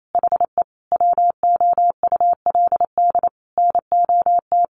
Morse code, heard as a keyed tone, about 28 wpm, 700 Hz. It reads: HE WOULD NOT
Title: Sentences from Top 600 Words 28wpm